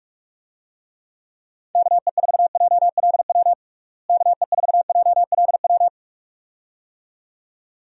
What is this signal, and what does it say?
Morse code, heard as a keyed tone, about 46 wpm, 700 Hz. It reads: KE4JLW KE4JLW
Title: USA Call Signs - 6 Characters - Repeated 2x - 45wpm